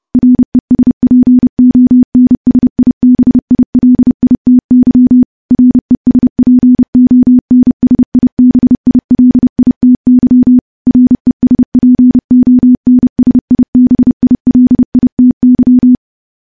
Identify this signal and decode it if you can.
Morse code, heard as a keyed tone, about 30 wpm, 250 Hz. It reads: RESPONSIBILITY RESPONSIBILITY RESPONSIBILITY
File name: morse / morse-training-30WPM